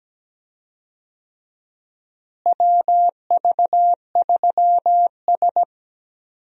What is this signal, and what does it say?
Morse code, heard as a keyed tone, about 17 wpm, 700 Hz. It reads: WV3S
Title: USA Call Signs - 4 Characters - Repeated 3x - 17wpm